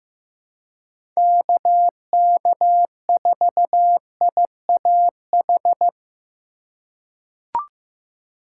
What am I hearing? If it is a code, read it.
Morse code, heard as a keyed tone, about 15 wpm, 700 Hz. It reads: KK4IAH
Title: USA Call Signs - 3x Character Spacing - 15wpm